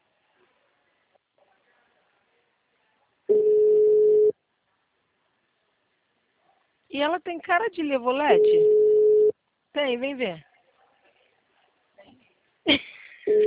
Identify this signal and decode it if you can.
Ringback tone (Europe, 425 Hz), with speech.
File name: out- -2016- - - .39816